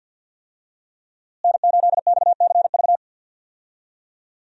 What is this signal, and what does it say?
Morse code, heard as a keyed tone, about 52 wpm, 700 Hz. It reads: N8XCV
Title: USA Call Signs - 5 Characters - Repeated 3x - 50wpm